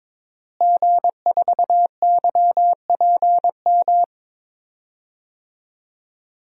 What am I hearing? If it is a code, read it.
Morse code, heard as a keyed tone, about 22 wpm, 700 Hz. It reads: G4YPM
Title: DXCC-100 Call Signs - 5 Characters - Repeated 2x - 22wpm